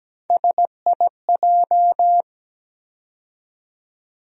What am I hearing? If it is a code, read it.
Morse code, heard as a keyed tone, about 17 wpm, 700 Hz. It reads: SIJ
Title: USA Call Sign Postfixes - 3 Letters 17wpm